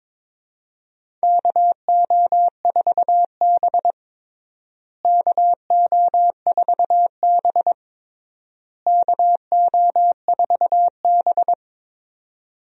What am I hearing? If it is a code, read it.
Morse code, heard as a keyed tone, about 22 wpm, 700 Hz. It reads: KO4B KO4B KO4B
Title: USA Call Signs - 4 Characters - Repeated 3x - 22wpm